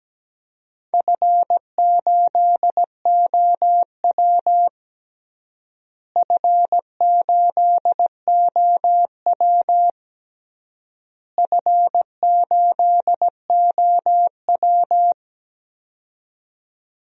Morse code, heard as a keyed tone, about 17 wpm, 700 Hz. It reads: F8OW F8OW F8OW
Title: DXCC-100 Call Signs - 4 Characters - Repeated 3x - 17wpm